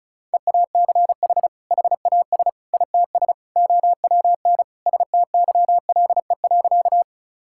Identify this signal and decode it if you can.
Morse code, heard as a keyed tone, about 34 wpm, 700 Hz. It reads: EACH HAS ITS OWN STYLE.
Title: Sentences from Top 1000 Words - Review Part 5 - 35wpm